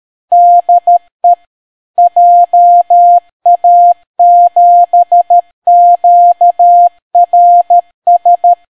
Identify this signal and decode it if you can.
Morse code, heard as a keyed tone, about 13 wpm, 700 Hz. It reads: DE JA7QRS